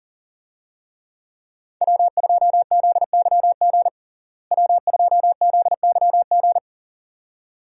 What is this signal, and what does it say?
Morse code, heard as a keyed tone, about 40 wpm, 700 Hz. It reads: W2ZYG W2ZYG
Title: USA Call Signs - 5 Characters - Repeated 2x - 40wpm